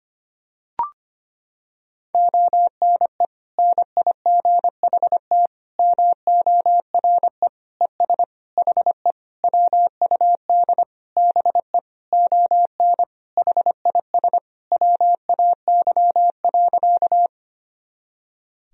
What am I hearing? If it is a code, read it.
Morse code, heard as a keyed tone, about 25 wpm, 700 Hz. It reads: ONE NIGHT MORE ES HE WUD BE ON HIS WAY.
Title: Sentences from Top 300 Words and Abbreviations 25wpm